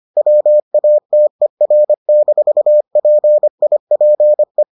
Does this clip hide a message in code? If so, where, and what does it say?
Yes: Morse code from 0.2 to 4.6 s, heard as a keyed tone, about 25 wpm, 600 Hz. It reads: WATER-PIPE